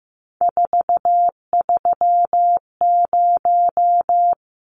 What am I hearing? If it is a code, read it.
Morse code, heard as a keyed tone, about 15 wpm, 700 Hz. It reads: 430
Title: Numbers - 3-Digit 15wpm